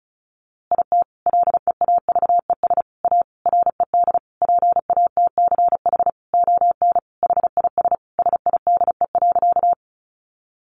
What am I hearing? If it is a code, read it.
Morse code, heard as a keyed tone, about 34 wpm, 700 Hz. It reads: IT LEAVES A RED PATCH ON HIS SIDE.
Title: A Sound Of Thunder 35wpm